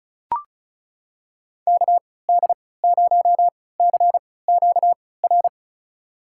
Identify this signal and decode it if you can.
Morse code, heard as a keyed tone, about 36 wpm, spaced as for 20 wpm, 700 Hz. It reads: KD0CQR
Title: USA Call Signs - 6 Characters - 3x Character Spacing - 35wpm